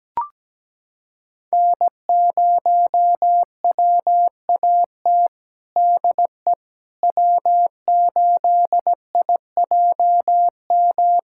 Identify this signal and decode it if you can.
Morse code, heard as a keyed tone, about 17 wpm, 700 Hz. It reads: N0WAT DE W8IJM
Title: USA Callsign DE Callsign - 5 Characters - 17wpm